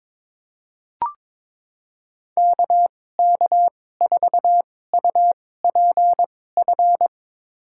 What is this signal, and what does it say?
Morse code, heard as a keyed tone, about 22 wpm, spaced as for 16 wpm, 700 Hz. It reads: KK4UPF
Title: USA Call Signs - 2x Character Spacing - 22wpm